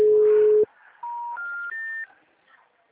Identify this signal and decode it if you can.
Special information tone.
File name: out- -2010- - - .64193